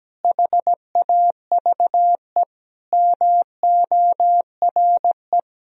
Morse code, heard as a keyed tone, about 17 wpm, 700 Hz. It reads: HAVE MORE
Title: Sentences from Top 1000 Words - Review Part 3 - 17wpm